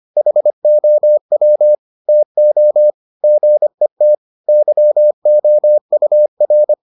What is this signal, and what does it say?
Morse code, heard as a keyed tone, about 25 wpm, 600 Hz. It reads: HOW TO GET YOUR